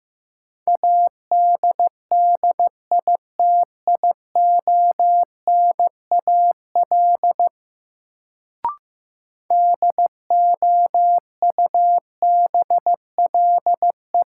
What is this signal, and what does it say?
Morse code, heard as a keyed tone, about 15 wpm, 700 Hz. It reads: ADDITIONAL DOUBLE
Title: Sentences from Top 1400 Words - New Word Drill Alt - 15wpm